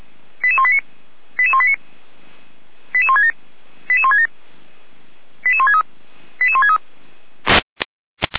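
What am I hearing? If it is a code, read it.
Five-tone selective call (ZVEI), heard as quick runs of five tones, tones of about 70 ms. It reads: 88189 88189 88167 88167 88163 88163
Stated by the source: ZVEI five-tone call received on an SDR.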